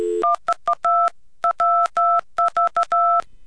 DTMF, heard as telephone touch tones, tones of about 130 ms. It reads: 1212 2222222